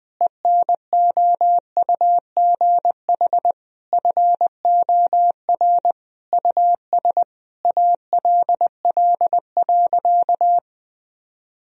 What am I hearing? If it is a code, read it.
Morse code, heard as a keyed tone, about 20 wpm, 700 Hz. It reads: ENOUGH FOR US ALL.
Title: Sentences from Top 1000 Words - Review Part 1 - 20wpm